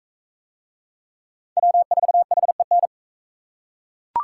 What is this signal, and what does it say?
Morse code, heard as a keyed tone, about 37 wpm, 700 Hz. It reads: W4HEN